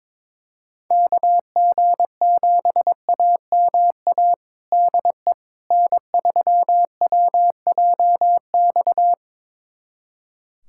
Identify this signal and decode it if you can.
Morse code, heard as a keyed tone, about 22 wpm, 700 Hz. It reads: KG7AMA DE N3WJX
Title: USA Callsign DE Callsign 22wpm